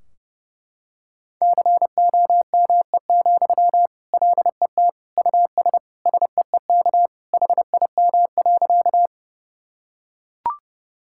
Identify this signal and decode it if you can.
Morse code, heard as a keyed tone, about 30 wpm, 700 Hz. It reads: COME, LET US SEEK HIM.
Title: Sentences from Top 1600 Words 30wpm